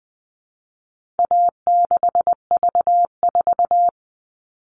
Morse code, heard as a keyed tone, about 20 wpm, 700 Hz. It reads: A6V4
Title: All Cut Numbers - 4-Digit 20wpm